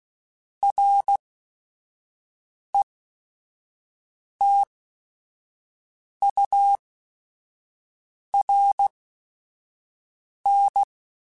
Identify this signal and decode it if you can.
Morse code, heard as a keyed tone, about 16 wpm, spaced as for 5 wpm, 800 Hz. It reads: RETURN